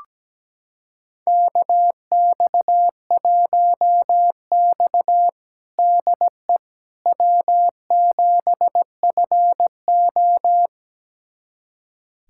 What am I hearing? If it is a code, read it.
Morse code, heard as a keyed tone, about 17 wpm, 700 Hz. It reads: KX1X DE W7FO
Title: USA Callsign DE Callsign - 4 Characters - 17wpm